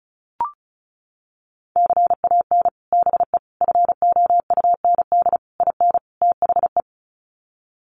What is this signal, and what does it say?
Morse code, heard as a keyed tone, about 34 wpm, 700 Hz. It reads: CAN BE FOUND IN THE
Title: Sets of 5 Words Encore 35wpm